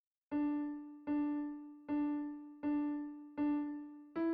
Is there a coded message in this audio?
No.